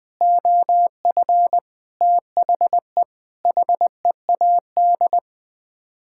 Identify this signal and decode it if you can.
Morse code, heard as a keyed tone, about 20 wpm, 700 Hz. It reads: OF THE HEAD
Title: Sets of 6 Words - 20wpm